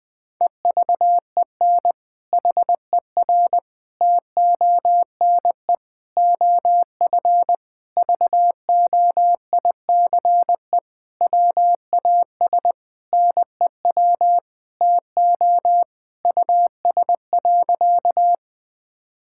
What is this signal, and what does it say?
Morse code, heard as a keyed tone, about 21 wpm, 700 Hz. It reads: EVEN HER TONE OF VOICE WAS NEW TO US.
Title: Sentences from Top 1600 Words 20wpm